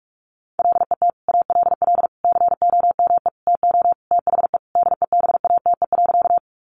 Morse code, heard as a keyed tone, about 46 wpm, 700 Hz. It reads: LET ALL COME TO THE DEBATE.
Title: Sentences from Top 1100 Words 45wpm